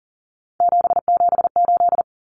Morse code, heard as a keyed tone, about 39 wpm, 700 Hz. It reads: 778
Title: Numbers - 3-Digit 40wpm